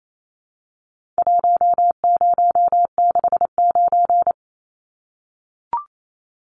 Morse code, heard as a keyed tone, about 28 wpm, 700 Hz. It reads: 1069
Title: Numbers - Contest Style - 4-Digit 28wpm